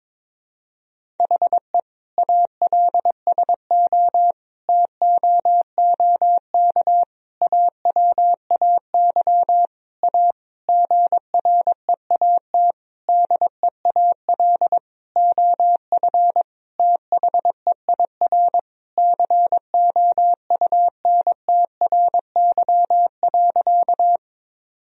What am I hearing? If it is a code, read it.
Morse code, heard as a keyed tone, about 22 wpm, 700 Hz. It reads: HE ALSO TOOK AWAY A GREAT DEAL OF THEIR COUNTRY.
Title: Sentences from Top 1000 Words - Review Part 1 - 22wpm